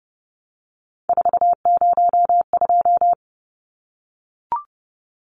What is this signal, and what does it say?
Morse code, heard as a keyed tone, about 30 wpm, 700 Hz. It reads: 402
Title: Numbers - 3-Digit 30wpm